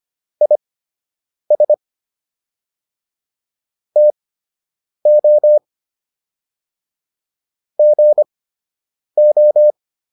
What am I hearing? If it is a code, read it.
Morse code, heard as a keyed tone, about 25 wpm, spaced as for 8 wpm, 600 Hz. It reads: IS TO GO